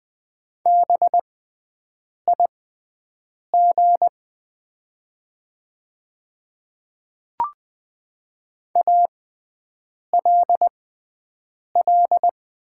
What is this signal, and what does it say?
Morse code, heard as a keyed tone, about 20 wpm, spaced as for 7 wpm, 700 Hz. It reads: BIG ALL
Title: 6x Character Spacing - Top 200 Words - 20wpm